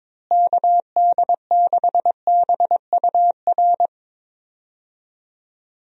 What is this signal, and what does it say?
Morse code, heard as a keyed tone, about 22 wpm, 700 Hz. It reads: KD6BUR
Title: USA Call Signs - 6 Characters - Repeated 2x - 22wpm